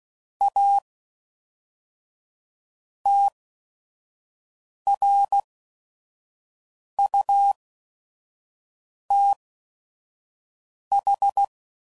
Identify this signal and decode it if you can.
Morse code, heard as a keyed tone, about 16 wpm, spaced as for 5 wpm, 800 Hz. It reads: A TRUTH
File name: CW Code Groups @5WPM / Technical Subjects / Gates Intro